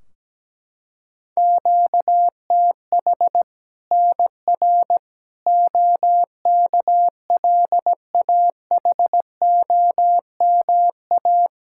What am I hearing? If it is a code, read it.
Morse code, heard as a keyed tone, about 17 wpm, 700 Hz. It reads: QTH NR OKLAHOMA
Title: Practice QSO - Exchange 1 - Beginning - 17wpm